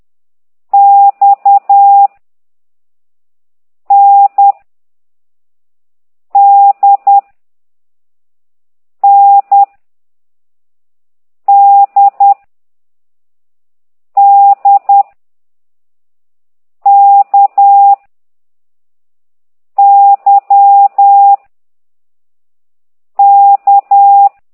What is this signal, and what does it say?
Morse code, heard as a keyed tone, about 10 wpm, 800 Hz. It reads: X N D N D D K Y K
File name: ct2idw / exerc4